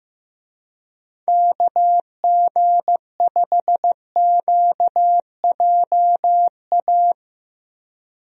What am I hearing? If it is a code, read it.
Morse code, heard as a keyed tone, about 15 wpm, 700 Hz. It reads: KG5QJA
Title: USA Call Signs - 6 Characters - Repeated 3x - 15wpm